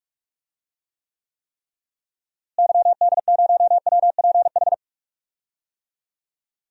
Morse code, heard as a keyed tone, about 43 wpm, 700 Hz. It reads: YD0WPH
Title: DXCC-100 Call Signs - 6 Characters - Repeated 3x - 45wpm